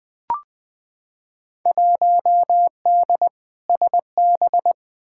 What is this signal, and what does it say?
Morse code, heard as a keyed tone, about 20 wpm, 700 Hz. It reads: 1D SB
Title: Field Day Exchanges - 20wpm